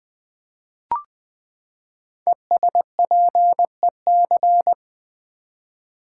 Morse code, heard as a keyed tone, about 20 wpm, 700 Hz. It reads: ESPEC
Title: Build A Word - 5 Letters At A Time - 10-Letter Words - 20wpm